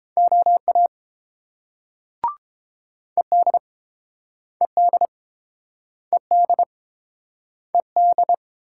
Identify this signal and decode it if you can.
Morse code, heard as a keyed tone, about 30 wpm, 700 Hz. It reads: OA ED ED ED ED